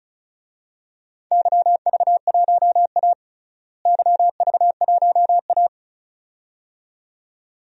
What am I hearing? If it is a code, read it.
Morse code, heard as a keyed tone, about 36 wpm, 700 Hz. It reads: YV1A YV1A
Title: DXCC-100 Call Signs - 4 Characters - Repeated 2x - 35wpm